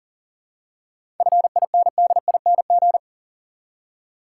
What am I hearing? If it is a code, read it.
Morse code, heard as a keyed tone, about 40 wpm, 700 Hz. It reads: FINDING
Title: Seven Letter Words - 40wpm